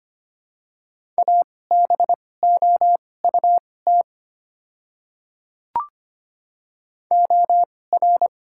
Morse code, heard as a keyed tone, about 25 wpm, spaced as for 18 wpm, 700 Hz. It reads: ABOUT OR
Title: 2x Character Spacing - Top 300 Words - 25wpm